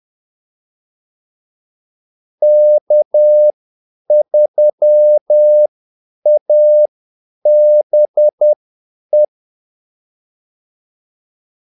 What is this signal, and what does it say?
Morse code, heard as a keyed tone, about 10 wpm, spaced as for 8 wpm, 600 Hz. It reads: K3ABE
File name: Callsigns-10-8wpm